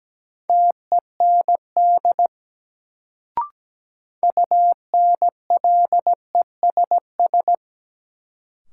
Morse code, heard as a keyed tone, about 17 wpm, 700 Hz. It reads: TEND UNLESS
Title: Sentences from Top 1000 Words - Review Part 5 - New Word Drill - 17wpm